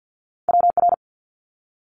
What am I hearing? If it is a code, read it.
Morse code, heard as a keyed tone, about 50 wpm, 700 Hz. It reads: WR
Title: Multi-letter Phonemes Quick Fire 50wpm